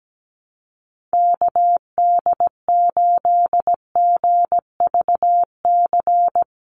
Morse code, heard as a keyed tone, about 17 wpm, 700 Hz. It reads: KD8GVC
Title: USA Call Signs 17wpm